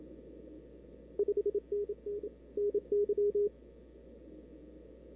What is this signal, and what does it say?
Morse code, heard as a keyed tone, about 30 wpm, 410 Hz. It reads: 5NN NY
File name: CW / 2005-k5tr arrl cw dx -03:00